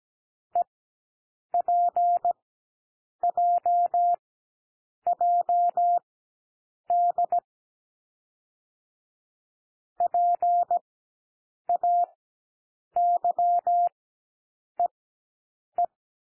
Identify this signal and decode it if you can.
Morse code, heard as a keyed tone, about 17 wpm, spaced as for 7 wpm, 700 Hz. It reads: EPJJD PAYEE
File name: SOUND LETTERS 17 wpm